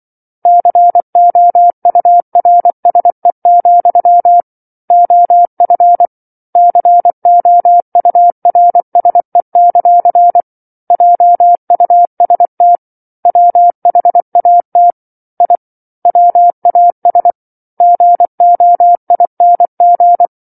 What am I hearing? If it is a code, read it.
Morse code, heard as a keyed tone, about 24 wpm, 700 Hz. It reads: COURSE, OF COURSE; JUST WHAT I WAS GOING